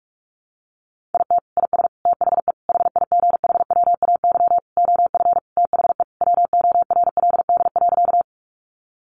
Morse code, heard as a keyed tone, about 44 wpm, 700 Hz. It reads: IT IS THE HIGHWAY OF THE WORLD.
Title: Sentences from Top 1400 Words 45wpm